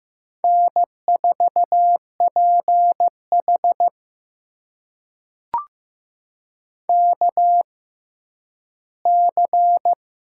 Morse code, heard as a keyed tone, about 15 wpm, spaced as for 10 wpm, 700 Hz. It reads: N4PH KC
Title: USA Call Signs - 6x Character Spacing - 15wpm